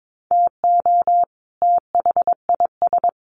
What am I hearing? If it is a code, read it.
Morse code, heard as a keyed tone, about 22 wpm, 700 Hz. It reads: TO THIS
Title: Sets of 2 Words 22wpm